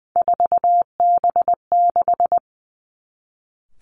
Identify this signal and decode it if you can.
Morse code, heard as a keyed tone, about 20 wpm, 700 Hz. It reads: 4B6
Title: Three Challenging Letter-Number Combinations 20wpm